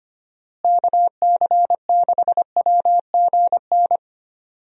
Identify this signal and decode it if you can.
Morse code, heard as a keyed tone, about 25 wpm, 700 Hz. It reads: KC6WGN
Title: USA Call Signs - 6 Characters - Repeated 2x - 25wpm